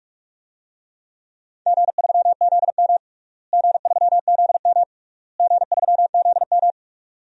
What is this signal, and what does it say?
Morse code, heard as a keyed tone, about 45 wpm, 700 Hz. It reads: G3ZM G3ZM G3ZM
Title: DXCC-100 Call Signs - 4 Characters - Repeated 3x - 45wpm